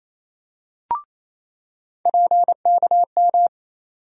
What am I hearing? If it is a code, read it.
Morse code, heard as a keyed tone, about 28 wpm, 700 Hz. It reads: PKM
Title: USA Call Sign Postfixes - 3 Letters 28wpm